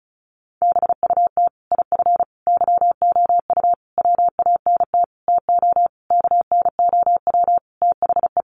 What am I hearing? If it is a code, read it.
Morse code, heard as a keyed tone, about 34 wpm, 700 Hz. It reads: BUT IF YOU WANT TO KNOW THE
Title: Sentences from Top 800 Words 35wpm